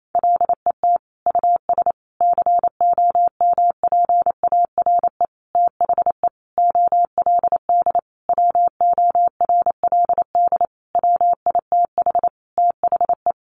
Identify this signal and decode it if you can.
Morse code, heard as a keyed tone, about 28 wpm, 700 Hz. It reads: LET US COMPARE THE OLD WORLD WITH THE
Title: Sentences from Top 1300 Words 28wpm